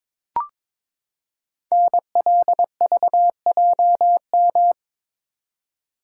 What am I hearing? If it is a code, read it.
Morse code, heard as a keyed tone, about 22 wpm, 700 Hz. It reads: NLVJM
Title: Code Groups - 5 Letters - 22wpm